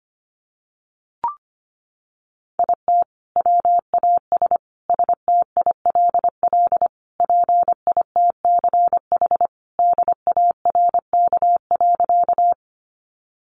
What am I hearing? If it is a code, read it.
Morse code, heard as a keyed tone, about 25 wpm, 700 Hz. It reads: IT WAS STILL PITCH DARK.
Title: Sentences from Top 1300 Words 25wpm